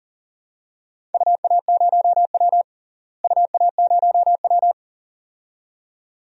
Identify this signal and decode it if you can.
Morse code, heard as a keyed tone, about 40 wpm, 700 Hz. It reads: UA0W UA0W
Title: DXCC-100 Call Signs - 4 Characters - Repeated 2x - 40wpm